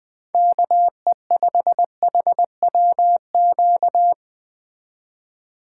Morse code, heard as a keyed tone, about 19 wpm, 700 Hz. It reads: KE5HWQ